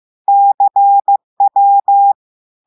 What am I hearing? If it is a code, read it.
Morse code, heard as a keyed tone, about 15 wpm, 800 Hz. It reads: CW
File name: eh1ard-amateur-radio-day-2016 mf feed 1